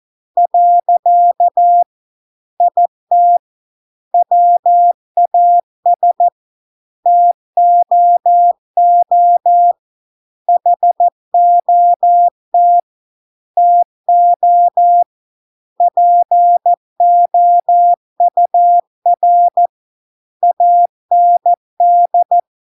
Morse code, heard as a keyed tone, about 14 wpm, 700 Hz. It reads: . IT WAS TOO HOT TO POUR AND